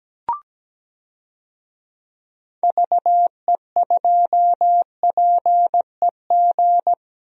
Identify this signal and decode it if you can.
Morse code, heard as a keyed tone, about 17 wpm, 700 Hz. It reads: VE2PEG
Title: DXCC-100 Call Signs - 6 Characters - 17wpm